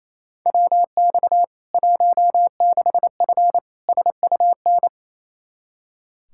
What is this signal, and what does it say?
Morse code, heard as a keyed tone, about 27 wpm, 700 Hz. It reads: WX 16F SUN
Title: K1USN SST QSO Practice 28wpm